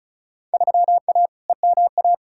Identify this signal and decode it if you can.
Morse code, heard as a keyed tone, about 34 wpm, 700 Hz. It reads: 3A EMA
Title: Field Day Exchanges - 35wpm